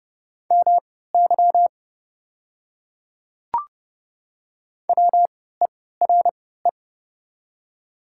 Morse code, heard as a keyed tone, about 30 wpm, spaced as for 17 wpm, 690 Hz. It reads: MY WERE
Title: 3x Character Spacing - Top 100 Words - 30wpm